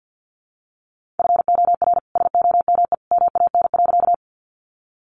Morse code, heard as a keyed tone, about 50 wpm, 700 Hz. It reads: FOR SOME MAN.
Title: Sentences from Top 200 Words 50wpm